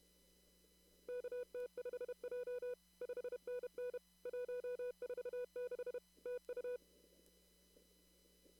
Morse code, heard as a keyed tone, about 32 wpm, 490 Hz. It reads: KT5J 5NN 146 TU
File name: CW / kt5j wpx cw-op-k5pi